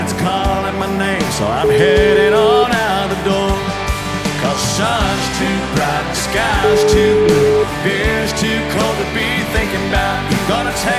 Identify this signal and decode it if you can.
Ringback tone (Europe, 425 Hz), with noise.